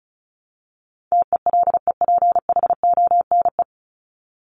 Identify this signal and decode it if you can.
Morse code, heard as a keyed tone, about 34 wpm, 700 Hz. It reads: TELEPHONE
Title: Top 1501-1750 Words 35wpm